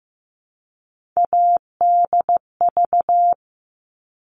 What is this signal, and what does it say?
Morse code, heard as a keyed tone, about 15 wpm, 700 Hz. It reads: ADV
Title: All Cut Numbers - 3-Digit 15wpm